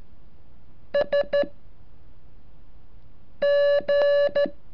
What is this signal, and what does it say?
Morse code, heard as a keyed tone, about 13 wpm, spaced as for 4 wpm, 570 Hz. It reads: SG
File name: curso / ejercicio 37